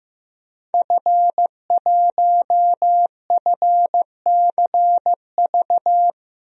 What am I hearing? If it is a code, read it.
Morse code, heard as a keyed tone, about 15 wpm, 700 Hz. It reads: F1FCV
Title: DXCC-100 Call Signs - 5 Characters - 15wpm